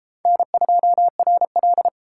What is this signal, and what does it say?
Morse code, heard as a keyed tone, about 29 wpm, 700 Hz. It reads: N2RL